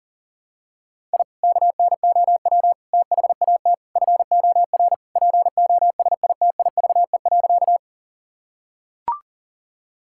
Morse code, heard as a keyed tone, about 40 wpm, 700 Hz. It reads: I KNOW THAT FOR POSITIVE.
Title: Sentences from Top 1000 Words - Review Part 5 - 40wpm